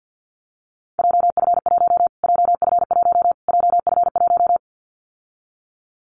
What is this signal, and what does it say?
Morse code, heard as a keyed tone, about 50 wpm, 700 Hz. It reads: JF1 JF1 JF1
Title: Callsign Prefixes for Top 25 DXCC - Repeated 3x - 50wpm